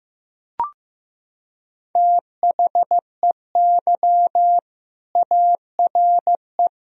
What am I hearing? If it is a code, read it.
Morse code, heard as a keyed tone, about 15 wpm, 700 Hz. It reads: THEY ARE
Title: Sentences from Top 1000 Words - Review Part 4 - 15wpm